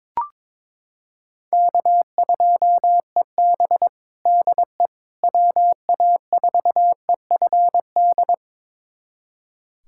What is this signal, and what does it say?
Morse code, heard as a keyed tone, about 21 wpm, 700 Hz. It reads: K2EB DE WA4EFD
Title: USA Callsign DE Callsign 22wpm